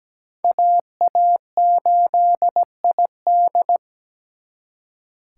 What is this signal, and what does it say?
Morse code, heard as a keyed tone, about 20 wpm, 700 Hz. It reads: AA8ID
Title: Speed Racing - USA 2x2 Call Signs 17wpm